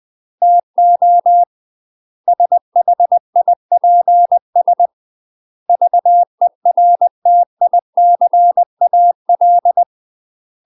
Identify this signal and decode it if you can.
Morse code, heard as a keyed tone, about 20 wpm, 700 Hz. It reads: TO SHIPS VERTICAL
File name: UK MASTER QSOs TRAINING / 20WPM / UK MASTER QSOs 0030